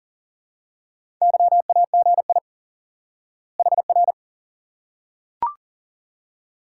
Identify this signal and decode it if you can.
Morse code, heard as a keyed tone, about 40 wpm, 700 Hz. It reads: YAGI HR
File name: course / instant-qso-element-course-3 / IQE Course 3 - Lesson 89 - YAGI - Two Elements - 40wpm